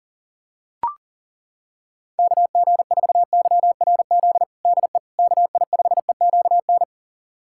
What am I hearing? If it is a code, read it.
Morse code, heard as a keyed tone, about 37 wpm, 690 Hz. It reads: KG4YRZ DE KI5EQN
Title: USA Callsign DE Callsign - 6 Characters - 40wpm